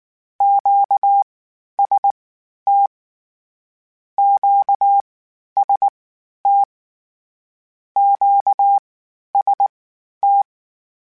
Morse code, heard as a keyed tone, about 19 wpm, spaced as for 11 wpm, 800 Hz. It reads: QST QST QST